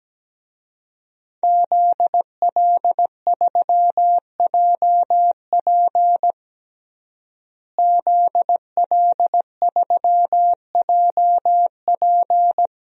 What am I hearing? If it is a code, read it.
Morse code, heard as a keyed tone, about 17 wpm, 700 Hz. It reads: ZL3JP ZL3JP
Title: DXCC-100 Call Signs - 5 Characters - Repeated 2x - 17wpm